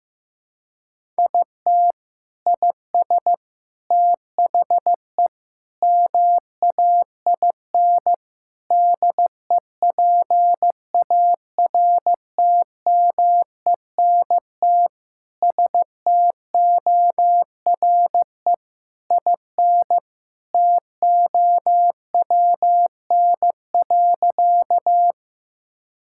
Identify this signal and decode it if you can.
Morse code, heard as a keyed tone, about 15 wpm, 700 Hz. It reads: IT IS THE MAIN DEPARTMENT STORE IN TOWN.
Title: Sentences from Top 1000 Words - Review Part 5 - 15wpm